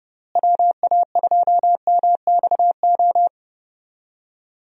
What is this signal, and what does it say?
Morse code, heard as a keyed tone, about 31 wpm, 700 Hz. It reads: WA2MXO
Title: USA Call Signs - 6 Characters - Repeated 2x - 30wpm